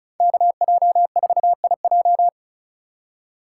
Morse code, heard as a keyed tone, about 36 wpm, 700 Hz. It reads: KJ4IJ
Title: USA Call Signs - 5 Characters - 2x Character Spacing - 35wpm